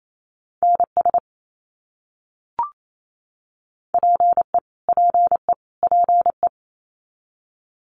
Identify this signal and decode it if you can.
Morse code, heard as a keyed tone, about 28 wpm, 700 Hz. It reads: NS PE PE PE
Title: Canadian Provinces and Territories - Repeated 3x - 28wpm